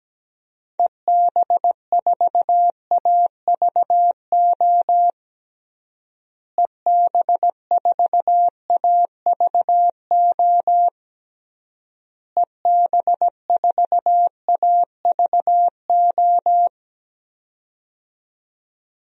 Morse code, heard as a keyed tone, about 17 wpm, 700 Hz. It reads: EB4AVO EB4AVO EB4AVO
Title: DXCC-100 Call Signs - 6 Characters - Repeated 3x - 17wpm